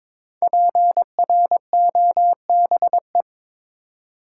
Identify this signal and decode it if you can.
Morse code, heard as a keyed tone, about 22 wpm, 700 Hz. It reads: PROBE
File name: 5 Letter Words - First 3 Letters Given - 22wpm